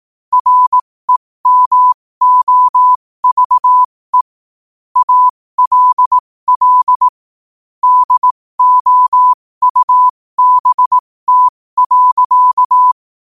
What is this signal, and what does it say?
Morse code, heard as a keyed tone, about 18 wpm, spaced as for 15 wpm, 1000 Hz. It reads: REMOVE ALL DOUBT.
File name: qotdfiles / 15wpm